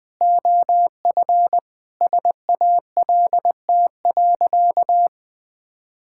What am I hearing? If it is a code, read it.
Morse code, heard as a keyed tone, about 20 wpm, 700 Hz. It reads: OF SALT.
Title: Sentences with 7-Word Phrases - 20wpm